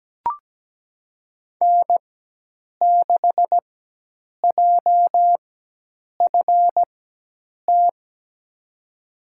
Morse code, heard as a keyed tone, about 17 wpm, spaced as for 8 wpm, 700 Hz. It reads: N6JFT
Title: USA Call Signs - 4x Character Spacing - 17wpm